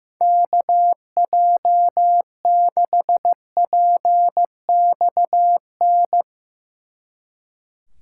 Morse code, heard as a keyed tone, about 16 wpm, 700 Hz. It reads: KJ6PXN
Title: Speed Racing - USA 2x3 Call Signs 15wpm